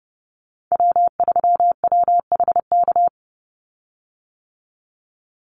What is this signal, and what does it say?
Morse code, heard as a keyed tone, about 30 wpm, 700 Hz. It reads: W3WHK
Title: Call Signs - 30wpm